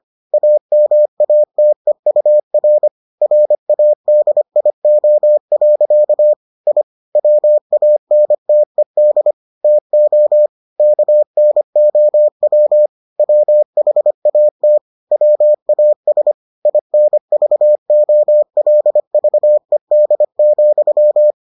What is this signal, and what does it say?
Morse code, heard as a keyed tone, about 25 wpm, 600 Hz. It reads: AMATEUR RADIO. I WANTED TO KNOW WHAT WAS INVOLVED,